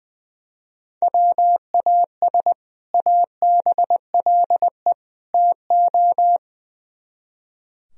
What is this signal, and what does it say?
Morse code, heard as a keyed tone, about 22 wpm, 700 Hz. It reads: WAS ABLE TO
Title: Speed-Racing - Sets of 3 Words 20wpm